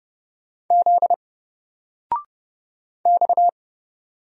Morse code, heard as a keyed tone, about 30 wpm, 700 Hz. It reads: Z X
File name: Single-Letter-Number-30wpm